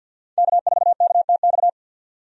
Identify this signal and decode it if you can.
Morse code, heard as a keyed tone, about 50 wpm, 700 Hz. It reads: K4KTX
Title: USA Call Signs - 5 Characters - 50wpm